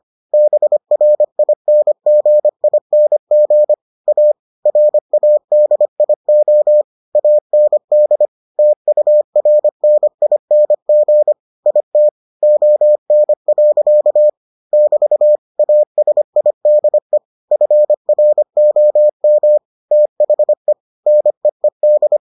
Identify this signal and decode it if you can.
Morse code, heard as a keyed tone, about 25 wpm, 600 Hz. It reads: BRINGING A RADIO AND TURNING IT ON. <BT> ASIDE FROM THE NEED